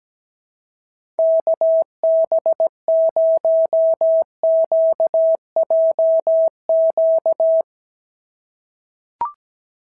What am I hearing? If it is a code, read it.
Morse code, heard as a keyed tone, about 17 wpm, 650 Hz. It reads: KB0QJQ
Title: SOTA QSO Practice 17wpm